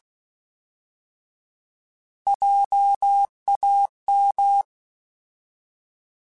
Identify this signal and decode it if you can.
Morse code, heard as a keyed tone, about 16 wpm, 800 Hz. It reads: JAM
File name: Alphabetical / Alphabetical mp3s / Js @15